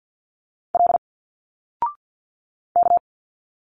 Morse code, heard as a keyed tone, about 50 wpm, 700 Hz. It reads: L K
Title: Single Letters 50wpm